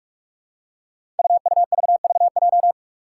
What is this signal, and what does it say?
Morse code, heard as a keyed tone, about 46 wpm, 700 Hz. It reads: UUVVJ
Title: Five Challenging Letter-Number Combinations 45wpm